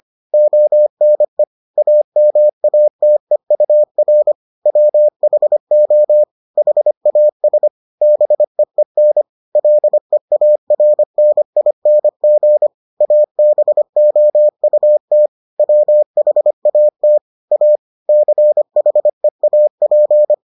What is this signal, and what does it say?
Morse code, heard as a keyed tone, about 25 wpm, 600 Hz. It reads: ONE AMATEUR WHO HAS BEEN LEARNING ABOUT WHAT A CHEAP